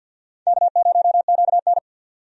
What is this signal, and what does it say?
Morse code, heard as a keyed tone, about 52 wpm, 700 Hz. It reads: K0QN
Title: USA Call Signs - 4 Characters - 50wpm